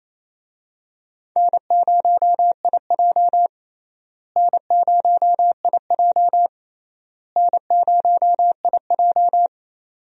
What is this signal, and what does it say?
Morse code, heard as a keyed tone, about 28 wpm, 700 Hz. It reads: N0IJ N0IJ N0IJ
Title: USA Call Signs - 4 Characters - Repeated 3x - 28wpm